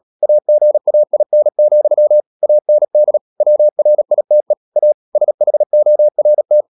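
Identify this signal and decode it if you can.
Morse code, heard as a keyed tone, about 37 wpm, 600 Hz. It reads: AGAIN, AND WRITE A SHORT
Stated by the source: Morse code practice at 35–39 wpm sending English text of Ken Follett's novel Never, from CW med Gnister.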